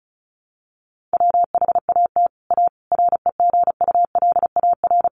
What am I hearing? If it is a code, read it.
Morse code, heard as a keyed tone, about 34 wpm, 700 Hz. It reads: WHAT A REGULAR
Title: Sentences from Top 900 Words 35wpm